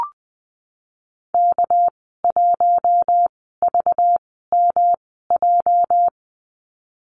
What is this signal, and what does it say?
Morse code, heard as a keyed tone, about 20 wpm, spaced as for 14 wpm, 700 Hz. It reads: K1VMJ
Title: 213 Lesson 34g - Number 8 - Call Signs 20wpm